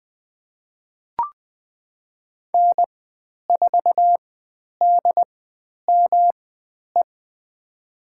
Morse code, heard as a keyed tone, about 20 wpm, spaced as for 10 wpm, 700 Hz. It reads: N4DME
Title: 109 Lesson 18g - Letter m - Call Signs 20wpm